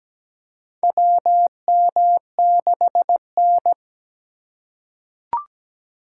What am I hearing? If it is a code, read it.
Morse code, heard as a keyed tone, about 17 wpm, 700 Hz. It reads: WM6N
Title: USA Call Signs - 2x Character Spacing - 17wpm